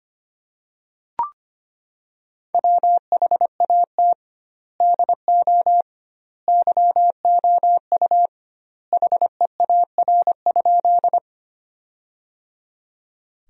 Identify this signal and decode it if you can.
Morse code, heard as a keyed tone, about 25 wpm, 700 Hz. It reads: WHAT DO YOU HEAR?
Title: 2x Word-Spacing - Sentences from Top 300 Words - 25wpm